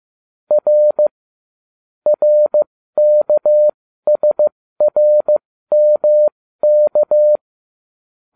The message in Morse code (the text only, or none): R RKSRMK